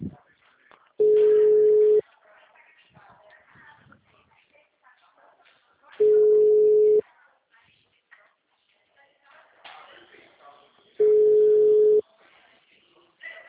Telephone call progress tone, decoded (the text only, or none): ringback tone